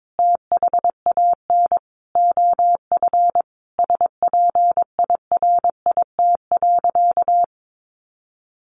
Morse code, heard as a keyed tone, about 22 wpm, 700 Hz. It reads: THAN OF SPIRIT.
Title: Sentences from Top 1400 Words 22wpm